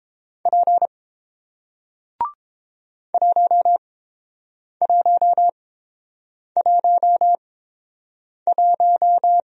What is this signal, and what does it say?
Morse code, heard as a keyed tone, about 30 wpm, spaced as for 7 wpm, 700 Hz. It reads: P 1111